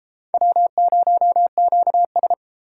Morse code, heard as a keyed tone, about 30 wpm, 700 Hz. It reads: W0QS